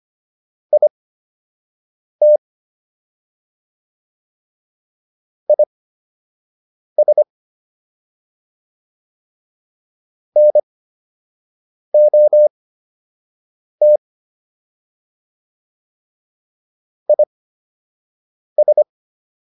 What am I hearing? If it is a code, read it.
Morse code, heard as a keyed tone, about 25 wpm, spaced as for 6 wpm, 600 Hz. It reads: IT IS NOT IS